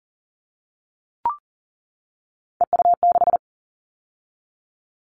Morse code, heard as a keyed tone, about 40 wpm, 700 Hz. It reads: EU6
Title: All Cut Numbers - 3-Digit 40wpm